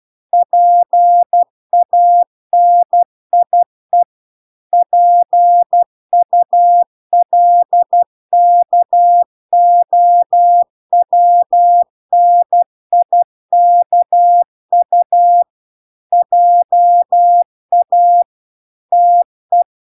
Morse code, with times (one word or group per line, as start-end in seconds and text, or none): 0.3-4.0 PANIE
4.7-15.4 PULKOWNIKU
16.1-18.2 JA
18.9-19.6 TE